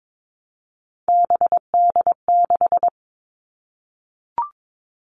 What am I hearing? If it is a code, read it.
Morse code, heard as a keyed tone, about 22 wpm, 700 Hz. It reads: BD6
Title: All Cut Numbers - 3-Digit 22wpm